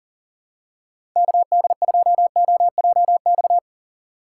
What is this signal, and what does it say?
Morse code, heard as a keyed tone, about 38 wpm, 700 Hz. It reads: KD2OJX
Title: USA Call Signs - 6 Characters - 4x Character Spacing - 40wpm